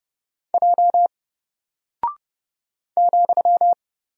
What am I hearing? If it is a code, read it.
Morse code, heard as a keyed tone, about 30 wpm, 700 Hz. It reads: J ,
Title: Single Character (Letter:Number:Punctuation:Prosign) Alt 30wpm